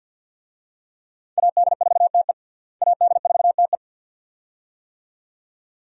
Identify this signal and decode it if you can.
Morse code, heard as a keyed tone, about 52 wpm, 700 Hz. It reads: AD4TE AD4TE
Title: USA Call Signs - 5 Characters - Repeated 2x - 50wpm